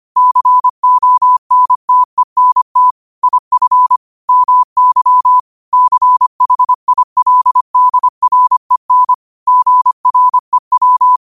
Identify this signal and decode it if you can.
Morse code, heard as a keyed tone, about 25 wpm, 1000 Hz. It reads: CONTENT IF MY CHILDREN GREW
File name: 25wpm